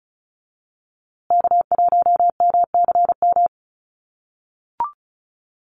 Morse code, heard as a keyed tone, about 34 wpm, 700 Hz. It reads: K1MCM
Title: USA Call Signs Repeated 3x 35wpm